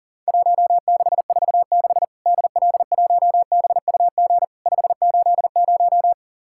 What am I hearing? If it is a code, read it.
Morse code, heard as a keyed tone, about 42 wpm, 700 Hz. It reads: 1646 DL1BUG 580
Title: Worked All Europe DX Contest - QTC Exchange - 40wpm